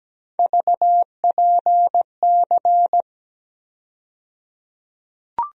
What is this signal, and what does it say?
Morse code, heard as a keyed tone, about 17 wpm, 700 Hz. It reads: VPC
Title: USA Call Sign Postfixes - 3 Letters 17wpm